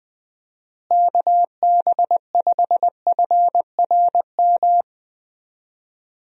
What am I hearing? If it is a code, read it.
Morse code, heard as a keyed tone, about 20 wpm, 700 Hz. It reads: KB5FRM
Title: USA Call Signs - 6 Characters - Repeated 2x - 20wpm